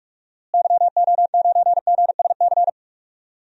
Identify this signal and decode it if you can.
Morse code, heard as a keyed tone, about 45 wpm, 700 Hz. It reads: YO9GSC
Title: DXCC-100 Call Signs - 6 Characters - Repeated 3x - 45wpm